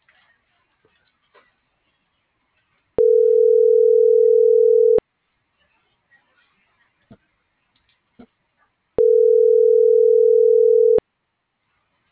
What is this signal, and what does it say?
Ringback tone (North America), 440 + 480 Hz.